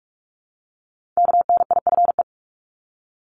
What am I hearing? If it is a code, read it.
Morse code, heard as a keyed tone, about 46 wpm, 700 Hz. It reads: KNIFE
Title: Multi-letter Phonemes Repeated 3x 45wpm